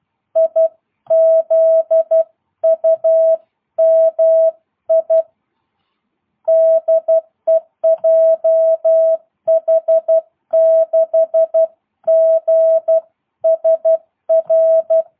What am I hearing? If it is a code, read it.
Morse code, heard as a keyed tone, about 12 wpm, 660 Hz. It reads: IZUMI DE JH6GSR
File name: jh8cbh / 2022cw